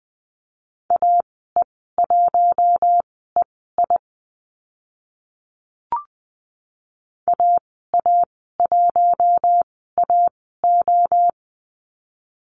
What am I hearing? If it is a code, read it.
Morse code, heard as a keyed tone, about 20 wpm, spaced as for 14 wpm, 700 Hz. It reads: AE1EI AA1AO
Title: 055 Lesson 10g - Letter r - Call Signs 20wpm